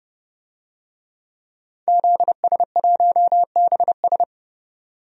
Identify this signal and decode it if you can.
Morse code, heard as a keyed tone, about 31 wpm, 700 Hz. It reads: ZS1BS
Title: DXCC-100 Call Signs - 5 Characters - 30wpm